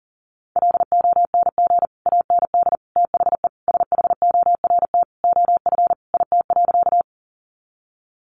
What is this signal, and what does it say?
Morse code, heard as a keyed tone, about 40 wpm, 700 Hz. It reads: LONG AND THE SHORT OF IT.
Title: Sentences from Top 400 Words 40wpm